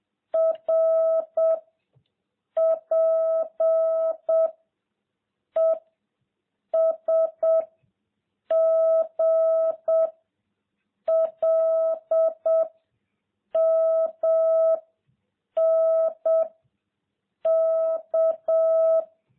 Morse code, heard as a keyed tone, about 7 wpm, 650 Hz. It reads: R P E S G L M N K